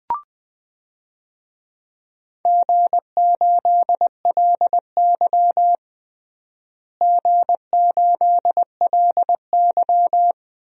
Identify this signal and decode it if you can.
Morse code, heard as a keyed tone, about 20 wpm, 700 Hz. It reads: G8LY G8LY
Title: DXCC-100 Call Signs - 4 Characters - Repeated 2x - 20wpm